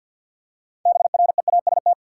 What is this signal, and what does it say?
Morse code, heard as a keyed tone, about 53 wpm, 700 Hz. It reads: BREAST
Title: Six Letter Words - 50wpm